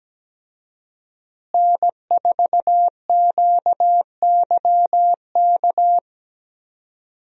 Morse code, heard as a keyed tone, about 17 wpm, 700 Hz. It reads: N4QYK
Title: USA Call Signs - 5 Characters - Repeated 2x - 17wpm